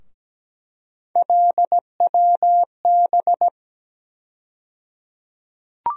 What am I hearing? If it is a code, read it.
Morse code, heard as a keyed tone, about 17 wpm, 700 Hz. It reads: LWB
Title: USA Call Sign Postfixes - 3 Letters 17wpm